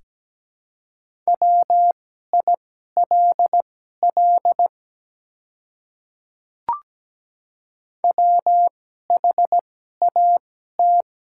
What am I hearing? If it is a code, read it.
Morse code, heard as a keyed tone, about 17 wpm, spaced as for 12 wpm, 700 Hz. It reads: WILL WHAT
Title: 2x Character Spacing - Top 100 Words - 17wpm